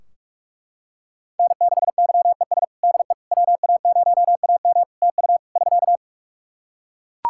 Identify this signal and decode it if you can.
Morse code, heard as a keyed tone, about 41 wpm, 700 Hz. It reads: N6YES DE WA0AM TU <SK>
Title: Practice QSO - Exchange 4 - Ending - 45wpm